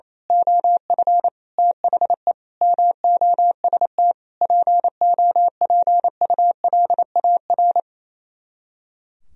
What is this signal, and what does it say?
Morse code, heard as a keyed tone, about 27 wpm, 700 Hz. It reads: OF THE MOST POPULAR
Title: Speed-Racing - Sets of 5 Words 28wpm